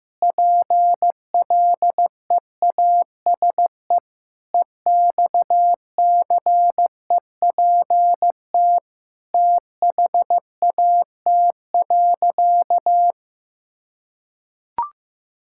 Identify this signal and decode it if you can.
Morse code, heard as a keyed tone, about 15 wpm, 700 Hz. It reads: PLEASE EXCEPT THAT.
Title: Sentences from Top 1600 Words 15wpm